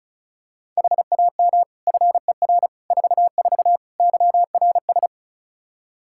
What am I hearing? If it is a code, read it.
Morse code, heard as a keyed tone, about 36 wpm, 700 Hz. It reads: HAM FER 44 YRS
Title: K1USN SST QSO Practice 35wpm